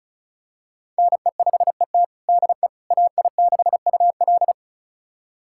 Morse code, heard as a keyed tone, about 36 wpm, 700 Hz. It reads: NE5ET DE AI6UL
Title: USA Callsign DE Callsign - 5 Characters - 35wpm